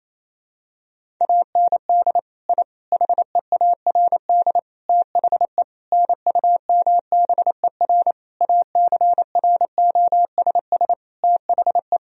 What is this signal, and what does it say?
Morse code, heard as a keyed tone, about 28 wpm, 700 Hz. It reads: AND I HEARD THE NUMBER ACROSS THE
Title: Sentences from Top 1000 Words - Review Part 1 - 28wpm